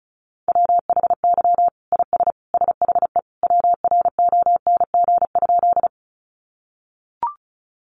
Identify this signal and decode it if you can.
Morse code, heard as a keyed tone, about 34 wpm, 700 Hz. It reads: WHY IS SHE WRONG?
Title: Sentences from Top 900 Words 35wpm